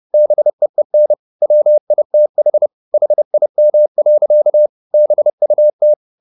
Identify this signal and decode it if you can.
Morse code, heard as a keyed tone, about 30 wpm, 600 Hz. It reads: BEEN WITH HIM. BUT
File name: the old man and the sea / the old man and the sea 1 30wpm